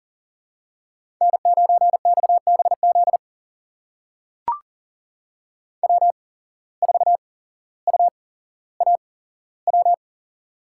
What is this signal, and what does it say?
Morse code, heard as a keyed tone, about 40 wpm, spaced as for 17 wpm, 700 Hz. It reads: N9XBZ W4UAW
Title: USA Call Signs - 5 Characters - 8x Character Spacing - 40wpm